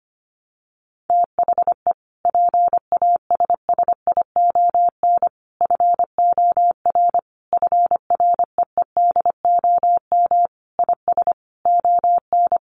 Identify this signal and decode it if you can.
Morse code, heard as a keyed tone, about 25 wpm, 700 Hz. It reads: THE PASSION FOR FREEDOM IS ON
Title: Sentences from Top 1300 Words 25wpm